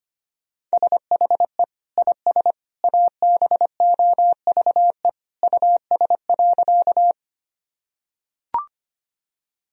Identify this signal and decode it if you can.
Morse code, heard as a keyed tone, about 25 wpm, 700 Hz. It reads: SHE IS ABOVE US.
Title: Sentences from Top 1000 Words - Review Part 2 - 25wpm